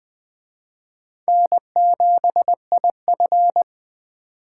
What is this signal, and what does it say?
Morse code, heard as a keyed tone, about 20 wpm, 700 Hz. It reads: N7IF
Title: Speed Racing - USA 1x2 Call Signs 17wpm